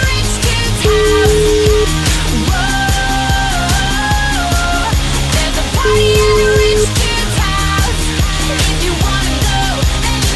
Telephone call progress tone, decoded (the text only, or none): ringback tone